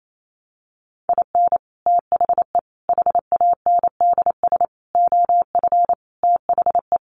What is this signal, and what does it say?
Morse code, heard as a keyed tone, about 28 wpm, 700 Hz. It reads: IN THE HANDS OF THE
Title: Sets of 5 Words 28wpm